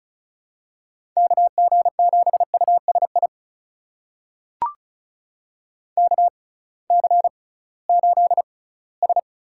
Morse code, heard as a keyed tone, about 36 wpm, spaced as for 20 wpm, 690 Hz. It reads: KG7USI KC8S
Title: USA Call Signs - 6 Characters - 6x Character Spacing - 35wpm